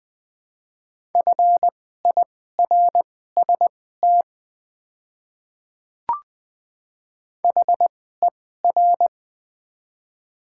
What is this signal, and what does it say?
Morse code, heard as a keyed tone, about 20 wpm, spaced as for 14 wpm, 700 Hz. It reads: FIRST HER
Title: 2x Character Spacing - Top 100 Words - 20wpm